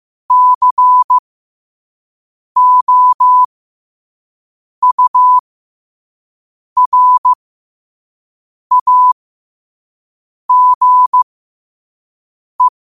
Morse code, heard as a keyed tone, about 15 wpm, spaced as for 5 wpm, 1000 Hz. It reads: COURAGE
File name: qotdfiles / 5wpm